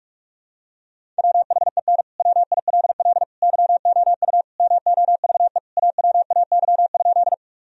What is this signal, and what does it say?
Morse code, heard as a keyed tone, about 41 wpm, 700 Hz. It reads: WHEN WILL YOU MOVE AWAY?
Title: Sentences from Top 1000 Words - Review Part 1 - 45wpm